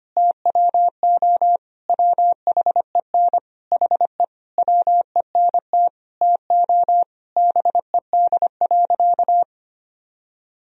Morse code, heard as a keyed tone, about 25 wpm, 700 Hz. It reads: TWO WHEN HE WENT TO BED.
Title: Sentences from Top 1000 Words - Review Part 3 - 25wpm